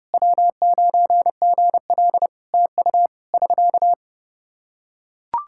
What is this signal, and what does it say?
Morse code, heard as a keyed tone, about 30 wpm, 700 Hz. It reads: W9GL TU <SK>
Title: Practice QSO - Exchange 4 - Ending - 30wpm